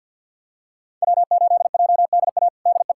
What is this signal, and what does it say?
Morse code, heard as a keyed tone, about 52 wpm, 700 Hz. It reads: W8JDA DE
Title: USA Callsign DE Callsign - 5 Characters - 50wpm